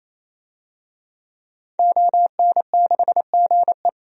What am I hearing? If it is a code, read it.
Morse code, heard as a keyed tone, about 28 wpm, 700 Hz. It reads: ON6GE
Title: DXCC-100 Call Signs - 5 Characters - 28wpm